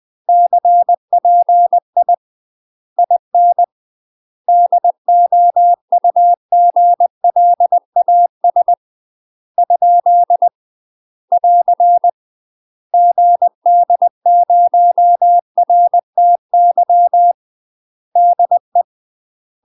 Morse code, heard as a keyed tone, about 20 wpm, 700 Hz. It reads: CPI IN DOUGLAS ? <AR> GD0RTY DE
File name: UK MASTER QSOs TRAINING / 20WPM / UK MASTER QSOs 0042